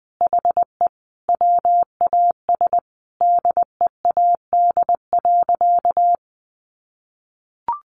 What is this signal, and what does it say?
Morse code, heard as a keyed tone, about 20 wpm, 700 Hz. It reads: HE WAS DEAD.
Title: Sentences from Top 1500 Words 20wpm